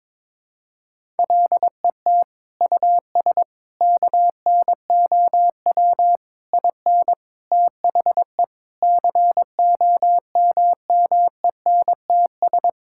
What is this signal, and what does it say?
Morse code, heard as a keyed tone, about 22 wpm, 700 Hz. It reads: LET US KNOW IN THE COMMENTS
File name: Sentences with 7-Word Phrases - 22wpm